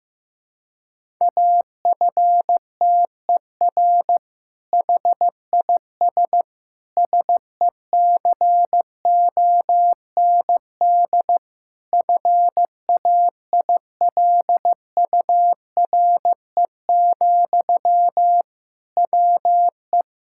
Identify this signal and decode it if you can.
Morse code, heard as a keyed tone, about 15 wpm, 700 Hz. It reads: AFTER HIS SECOND FAILURE, WE
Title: Sentences from Top 1000 Words - Review Part 6 - 15wpm